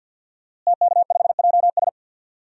Morse code, heard as a keyed tone, about 51 wpm, 700 Hz. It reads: TK5JS
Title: DXCC-100 Call Signs - 5 Characters - Repeated 3x - 50wpm